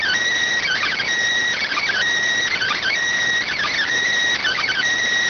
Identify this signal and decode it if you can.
Emergency Alert System (SAME) tones, heard as harsh buzzing bursts. It is heard from 0.6 to 4.8 s.